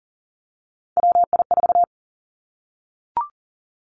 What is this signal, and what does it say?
Morse code, heard as a keyed tone, about 40 wpm, 700 Hz. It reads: WI4
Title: Three Challenging Letter-Number Combinations 40wpm